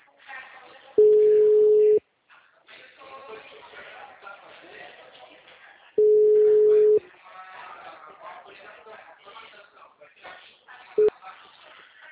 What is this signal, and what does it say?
Ringback tone (Europe, 425 Hz), with speech.